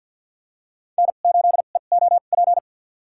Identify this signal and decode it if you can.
Morse code, heard as a keyed tone, about 49 wpm, spaced as for 35 wpm, 700 Hz. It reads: N8EOP
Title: USA Call Signs - 5 Characters - 2x Character Spacing - 50wpm